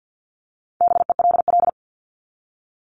Morse code, heard as a keyed tone, about 50 wpm, 700 Hz. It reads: BELL
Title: Top 1751-2000 Words 50wpm